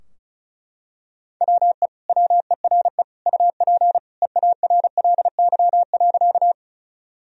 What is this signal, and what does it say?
Morse code, heard as a keyed tone, about 34 wpm, 700 Hz. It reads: WE WERE UP EARLY.
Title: Sentences from Top 1000 Words - Review Part 2 - 35wpm